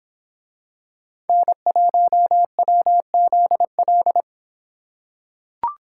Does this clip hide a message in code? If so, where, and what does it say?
Yes: Morse code from 1.3 to 4.2 s, heard as a keyed tone, about 26 wpm, 700 Hz. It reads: N1WZL